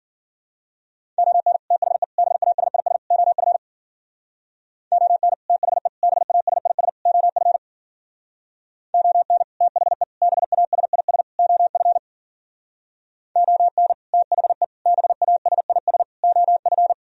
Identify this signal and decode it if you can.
Morse code, heard as a keyed tone, about 50 wpm, 700 Hz. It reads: ON THE BASIS OF ON THE BASIS OF ON THE BASIS OF ON THE BASIS OF
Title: Speed-Racing - Sets of 4 Words 40wpm